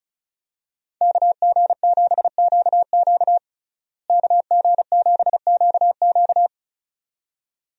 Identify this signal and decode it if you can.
Morse code, heard as a keyed tone, about 35 wpm, 700 Hz. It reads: KG7QQ KG7QQ
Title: USA Call Signs - 5 Characters - Repeated 2x - 35wpm